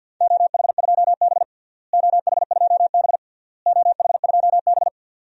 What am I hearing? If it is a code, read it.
Morse code, heard as a keyed tone, about 47 wpm, 700 Hz. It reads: OH2B OH2B OH2B
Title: DXCC-100 Call Signs - 4 Characters - Repeated 3x - 50wpm